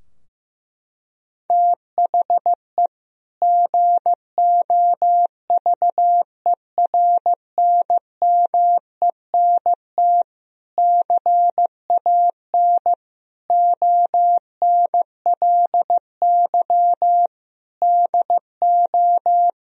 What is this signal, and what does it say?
Morse code, heard as a keyed tone, about 16 wpm, 700 Hz. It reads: THE GOVERNMENT CAN ONLY DO
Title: Sentences from Top 1000 Words - Review Part 3 - 15wpm